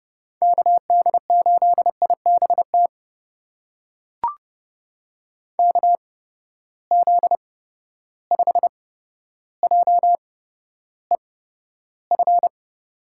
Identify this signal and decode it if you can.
Morse code, heard as a keyed tone, about 32 wpm, spaced as for 13 wpm, 700 Hz. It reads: KD8IBT KZ5JEF
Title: USA Call Signs - 6 Characters - 8x Character Spacing - 30wpm